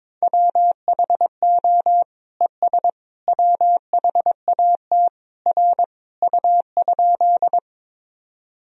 Morse code, heard as a keyed tone, about 22 wpm, 700 Hz. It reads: WHO ES WHAT R U?
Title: Sentences from Top 100 Words and Abbreviations 22wpm